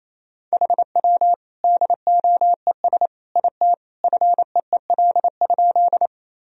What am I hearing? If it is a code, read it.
Morse code, heard as a keyed tone, about 28 wpm, 700 Hz. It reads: HW DOES IT FEEL?
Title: Sentences from Top 400 Words and Abbreviations 28wpm